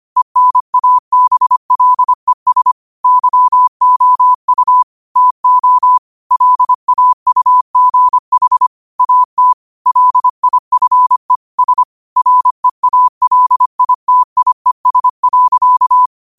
Morse code, heard as a keyed tone, about 25 wpm, 1000 Hz. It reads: ENABLES YOU TO LAUGH AT LIFE S REALITIES.